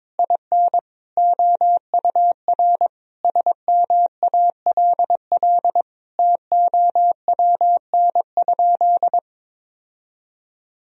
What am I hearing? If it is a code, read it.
Morse code, heard as a keyed tone, about 22 wpm, 700 Hz. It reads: IN OUR SMALL TOWN?
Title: Sentences from Top 1000 Words - Review Part 4 - 22wpm